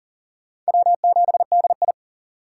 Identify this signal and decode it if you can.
Morse code, heard as a keyed tone, about 39 wpm, 700 Hz. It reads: W7DI
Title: USA Call Signs - 2x Character Spacing - 40wpm